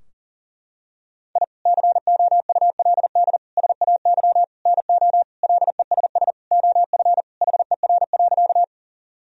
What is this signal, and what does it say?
Morse code, heard as a keyed tone, about 40 wpm, 700 Hz. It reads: I COULD SAY NO LESS OF HER.
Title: Sentences from Top 1000 Words - Review Part 2 - 40wpm